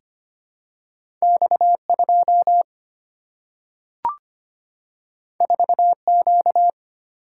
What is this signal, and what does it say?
Morse code, heard as a keyed tone, about 25 wpm, 700 Hz. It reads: X2 4Q
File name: Two Letter-Number Combinations - 25wpm